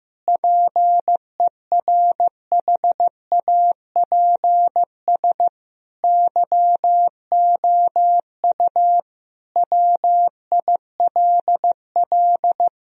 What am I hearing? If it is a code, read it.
Morse code, heard as a keyed tone, about 15 wpm, 700 Hz. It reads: PERHAPS YOU WILL
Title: Sentences from Top 1000 Words - Review Part 1 - 15wpm